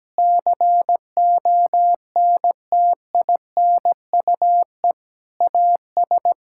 Morse code, heard as a keyed tone, about 17 wpm, 700 Hz. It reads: CONTINUE AS
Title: Sentences from Top 1000 Words - Review Part 6 - 17wpm